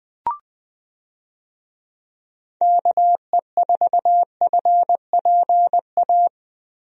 Morse code, heard as a keyed tone, about 20 wpm, 700 Hz. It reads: KE4FPA
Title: USA Call Signs - 6 Characters - Repeated 3x - 20wpm